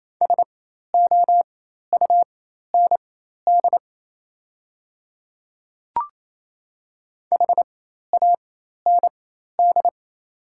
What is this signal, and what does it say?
Morse code, heard as a keyed tone, about 28 wpm, spaced as for 13 wpm, 690 Hz. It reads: SOUND HAND
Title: 4x Character Spacing - Top 300 Words - 28wpm